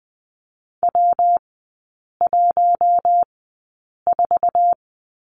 Morse code, heard as a keyed tone, about 20 wpm, spaced as for 8 wpm, 690 Hz. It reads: W14
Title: 159 Lesson 26e - Letter g - Three Characters 20wpm